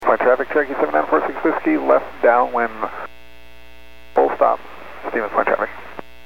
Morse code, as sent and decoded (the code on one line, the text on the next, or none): none